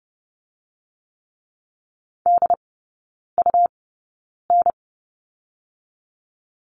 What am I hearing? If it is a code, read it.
Morse code, heard as a keyed tone, about 30 wpm, spaced as for 9 wpm, 700 Hz. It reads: DUN
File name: Lesson 17e - Letter u - Three Characters 30wpm